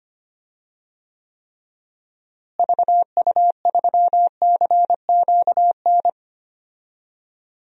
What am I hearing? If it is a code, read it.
Morse code, heard as a keyed tone, about 25 wpm, 690 Hz. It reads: VU3CQN
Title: DXCC-100 Call Signs - 6 Characters - Repeated 3x - 25wpm